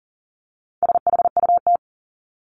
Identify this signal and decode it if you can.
Morse code, heard as a keyed tone, about 40 wpm, 700 Hz. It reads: SHUT
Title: Top 1751-2000 Words 40wpm